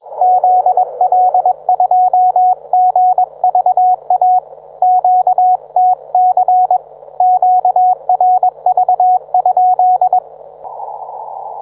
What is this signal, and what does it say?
Morse code, heard as a keyed tone, about 21 wpm, 730 Hz, with radio noise. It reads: ZL2GVA QTC QRV?
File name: book